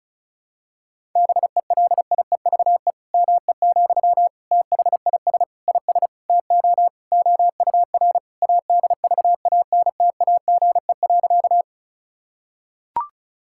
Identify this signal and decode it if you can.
Morse code, heard as a keyed tone, about 34 wpm, 700 Hz. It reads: BELIEVE ME, THIS IS TO OUR ADVANTAGE.
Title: Sentences from Top 1000 Words - Review Part 6 - 35wpm